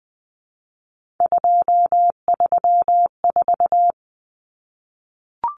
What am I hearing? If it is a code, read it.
Morse code, heard as a keyed tone, about 20 wpm, 700 Hz. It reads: 234
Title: Numbers - 3-Digit 20wpm